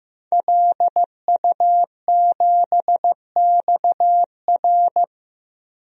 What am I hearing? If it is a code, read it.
Morse code, heard as a keyed tone, about 15 wpm, 700 Hz. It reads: LU7XR
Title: DXCC-100 Call Signs - 5 Characters - Repeated 3x - 15wpm